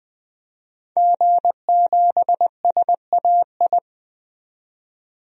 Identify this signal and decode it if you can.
Morse code, heard as a keyed tone, about 20 wpm, 700 Hz. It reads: G7SAI
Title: DXCC-100 Call Signs - 5 Characters - 20wpm